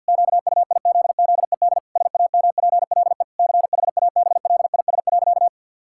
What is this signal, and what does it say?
Morse code, heard as a keyed tone, about 50 wpm, 700 Hz. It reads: QUIZZED SAMPLE CHABLIS.
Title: Pangram Challenge v2 - 50wpm